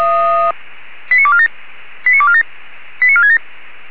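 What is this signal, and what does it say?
Five-tone selective call (ZVEI), heard as quick runs of five tones, tones of about 70 ms. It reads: 89247 89247 89467
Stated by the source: ZVEI five-tone call 89288 received by SDR.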